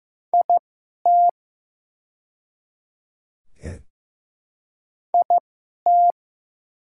Morse code, heard as a keyed tone, about 15 wpm, spaced as for 11 wpm, 700 Hz. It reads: IT IT
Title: 2x Character Spacing - Top 100 Words - 15wpm